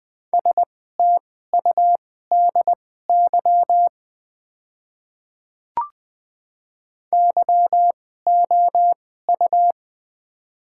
Morse code, heard as a keyed tone, about 20 wpm, spaced as for 14 wpm, 700 Hz. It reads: STUDY YOU
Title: 2x Character Spacing - Top 300 Words - 20wpm